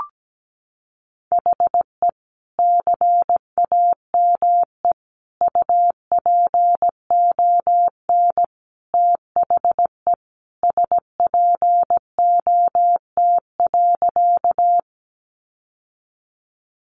Morse code, heard as a keyed tone, about 17 wpm, 700 Hz. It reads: HE CAME UPON THE SPOT.
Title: Sentences from Top 1100 Words 17wpm